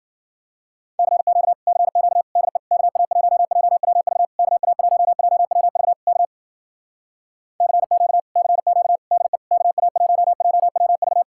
Morse code, heard as a keyed tone, about 53 wpm, 700 Hz. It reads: CQ CQ DE KA1JWV KA1JWV K CQ CQ DE KA1JWV